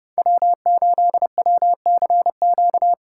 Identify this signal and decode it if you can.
Morse code, heard as a keyed tone, about 30 wpm, 700 Hz. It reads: W8WCQ
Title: USA Call Signs - 5 Characters - 2x Character Spacing - 30wpm